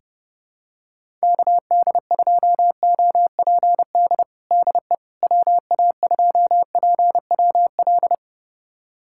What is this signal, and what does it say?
Morse code, heard as a keyed tone, about 30 wpm, 700 Hz. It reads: KD2OPD DE WA2PWL
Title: USA Callsign DE Callsign 30wpm